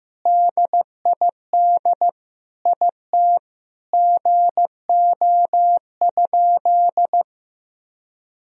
Morse code, heard as a keyed tone, about 15 wpm, 700 Hz. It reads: DID IT GO?
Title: Sentences from Top 200 Words and Abbreviations 15wpm